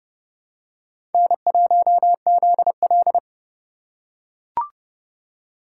Morse code, heard as a keyed tone, about 29 wpm, 700 Hz. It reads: N1ZL